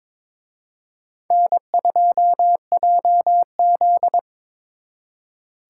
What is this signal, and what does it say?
Morse code, heard as a keyed tone, about 26 wpm, 700 Hz. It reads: N2JZ
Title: Speed Racing - USA 1x2 Call Signs 22wpm